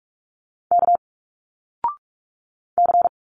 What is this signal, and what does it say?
Morse code, heard as a keyed tone, about 46 wpm, 700 Hz. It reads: K C
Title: Single Letters 45wpm